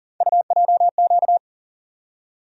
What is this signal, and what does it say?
Morse code, heard as a keyed tone, about 40 wpm, 700 Hz. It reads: UJQ
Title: USA Call Sign Postfixes - 3 Letters 40wpm